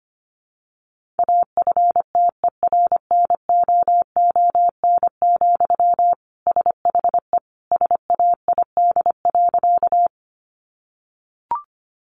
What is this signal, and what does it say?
Morse code, heard as a keyed tone, about 25 wpm, 700 Hz. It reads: AFTERNOON, SHE SAID.
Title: Sentences from Top 1500 Words 25wpm